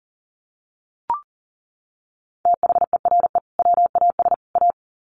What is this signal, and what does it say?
Morse code, heard as a keyed tone, about 40 wpm, 700 Hz. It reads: THERE WAS A
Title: Sets of 3 Words 40wpm